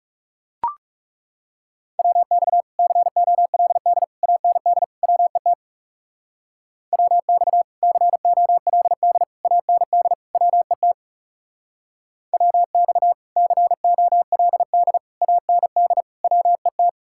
Morse code, heard as a keyed tone, about 35 wpm, 700 Hz. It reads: WX COLD AND WET WX COLD AND WET WX COLD AND WET
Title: Speed-Racing Common QSO Phrases Encore - 30wpm